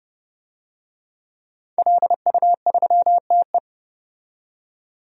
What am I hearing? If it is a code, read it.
Morse code, heard as a keyed tone, about 31 wpm, 700 Hz. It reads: LU3TE
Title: DXCC-100 Call Signs - 5 Characters - 30wpm